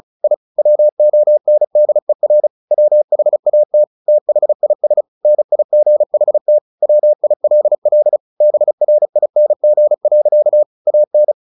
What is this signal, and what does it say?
Morse code, heard as a keyed tone, about 34 wpm, 600 Hz. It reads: I WONDER WHAT THIS NIGHT WILL BRING. AN
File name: the old man and the sea 2 35wpm